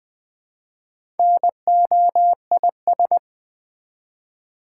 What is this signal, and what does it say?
Morse code, heard as a keyed tone, about 20 wpm, 700 Hz. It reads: NOIS
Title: Build A Word - 1 Letter At A Time - 5-Letter Words - 20wpm